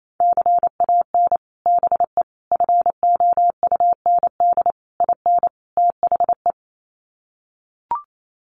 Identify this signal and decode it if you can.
Morse code, heard as a keyed tone, about 28 wpm, 700 Hz. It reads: CAN BE FOUND IN THE
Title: Sets of 5 Words Encore 28wpm